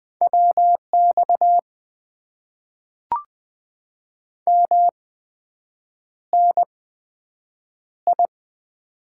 Morse code, heard as a keyed tone, about 20 wpm, spaced as for 7 wpm, 700 Hz. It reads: WX MNI
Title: Top 90 QSO Elements - 8x Character Spacing - 20wpm